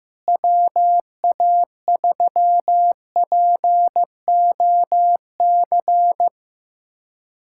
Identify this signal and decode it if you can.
Morse code, heard as a keyed tone, about 15 wpm, 700 Hz. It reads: WA3POC
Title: USA Callsign DE Callsign 15wpm